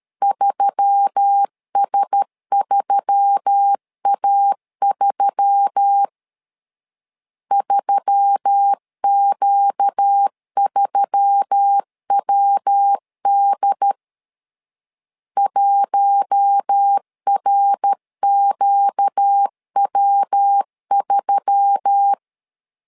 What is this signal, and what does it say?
Morse code, heard as a keyed tone, about 13 wpm, 800 Hz. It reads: 3S3A3 3Q3WD 1RQW3
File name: LEZIONI / 4 Q1S3 12wpm 2023